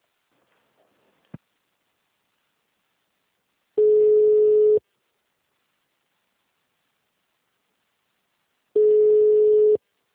Ringback tone (Europe, 425 Hz).